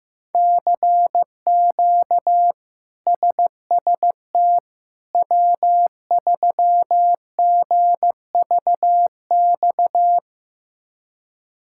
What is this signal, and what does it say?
Morse code, heard as a keyed tone, about 15 wpm, 700 Hz. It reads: CQ SST W3GVX
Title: K1USN SST QSO Practice - 15wpm